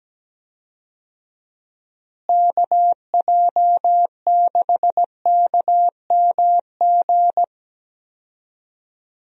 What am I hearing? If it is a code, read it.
Morse code, heard as a keyed tone, about 17 wpm, 700 Hz. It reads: KJ6KMG
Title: USA Call Signs - 6 Characters - Repeated 2x - 17wpm